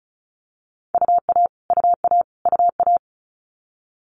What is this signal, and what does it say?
Morse code, heard as a keyed tone, about 36 wpm, 700 Hz. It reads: UA UA UA
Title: Multi-letter Phonemes Repeated 3x 35wpm